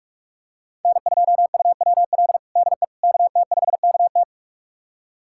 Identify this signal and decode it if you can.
Morse code, heard as a keyed tone, about 46 wpm, 700 Hz. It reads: N2UWL DE KT5KT
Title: USA Callsign DE Callsign - 5 Characters - 45wpm